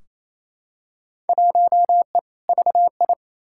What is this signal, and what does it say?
Morse code, heard as a keyed tone, about 28 wpm, 700 Hz. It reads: 1E VI
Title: Field Day Exchanges Encore - 28wpm